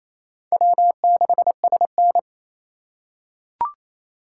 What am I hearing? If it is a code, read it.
Morse code, heard as a keyed tone, about 27 wpm, 700 Hz. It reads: W6SN
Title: USA Call Signs - 3x Character Spacing - 28wpm